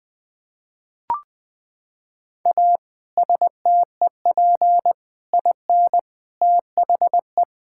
Morse code, heard as a keyed tone, about 20 wpm, 700 Hz. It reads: A STEP IN THE
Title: Sets of 6 Words - 20wpm